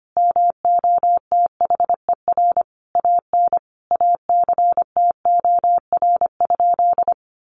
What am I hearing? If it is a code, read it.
Morse code, heard as a keyed tone, about 25 wpm, 700 Hz. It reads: MOTHER AN ACTOR?
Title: Sentences from Top 1500 Words 25wpm